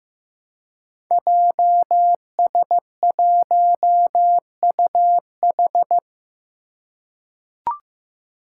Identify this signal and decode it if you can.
Morse code, heard as a keyed tone, about 15 wpm, 700 Hz. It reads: JS1UH
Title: Five Challenging Letter-Number Combinations 15wpm